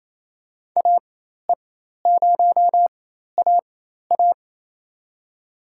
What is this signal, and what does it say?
Morse code, heard as a keyed tone, about 28 wpm, spaced as for 13 wpm, 700 Hz. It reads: AE0AA
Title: USA Call Signs - 5 Characters - 4x Character Spacing - 28wpm